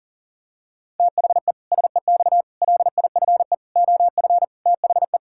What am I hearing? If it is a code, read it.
Morse code, heard as a keyed tone, about 40 wpm, 700 Hz. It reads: THE SEX LIFE OF THE
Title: Pangram Challenge v2 - 40wpm